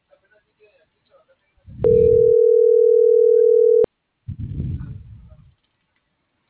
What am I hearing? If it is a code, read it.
Ringback tone (North America), 440 + 480 Hz.
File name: CENTRAL MULTIMODAL / out viridiana.juarez